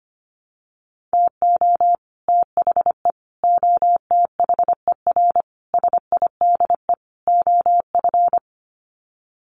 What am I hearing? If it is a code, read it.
Morse code, heard as a keyed tone, about 25 wpm, 700 Hz. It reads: TO THE OTHER SIDE OF
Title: Sets of 5 Words Encore 25wpm